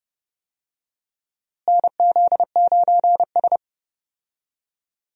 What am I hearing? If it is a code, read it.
Morse code, heard as a keyed tone, about 30 wpm, 700 Hz. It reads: NZ9S
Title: USA Call Signs - 4 Characters - Repeated 3x - 30wpm